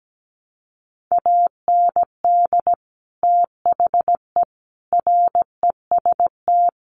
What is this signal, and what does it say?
Morse code, heard as a keyed tone, about 17 wpm, 700 Hz. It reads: AND THE REST
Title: Sets of 4 Words 17wpm